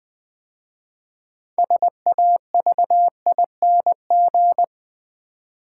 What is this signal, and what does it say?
Morse code, heard as a keyed tone, about 20 wpm, 700 Hz. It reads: SAVING
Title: 6 Letter Words - First 3 Letters Given - 20wpm